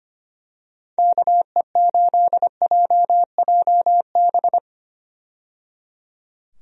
Morse code, heard as a keyed tone, about 25 wpm, 700 Hz. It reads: KE8JJB
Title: USA Call Signs - 6 Characters - 25wpm